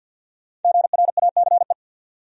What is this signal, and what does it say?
Morse code, heard as a keyed tone, about 53 wpm, 700 Hz. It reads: GRACE
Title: USA Common Names - 50wpm